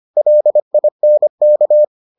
Morse code, heard as a keyed tone, about 25 wpm, 600 Hz. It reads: LINK